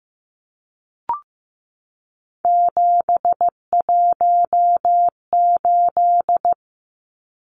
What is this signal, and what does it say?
Morse code, heard as a keyed tone, about 15 wpm, 700 Hz. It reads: 718
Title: Numbers - 3-Digit 15wpm